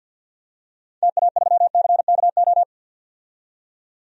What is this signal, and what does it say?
Morse code, heard as a keyed tone, about 52 wpm, 700 Hz. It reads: TA3CKO
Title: DXCC-100 Call Signs - 6 Characters - 50wpm